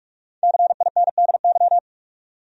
Morse code, heard as a keyed tone, about 46 wpm, 700 Hz. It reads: CINDY
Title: USA Common Names - 45wpm